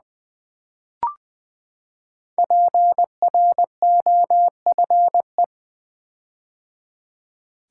Morse code, heard as a keyed tone, about 20 wpm, 700 Hz. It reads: PROFE
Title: Build A Word - 5 Letters At A Time - 10-Letter Words - 20wpm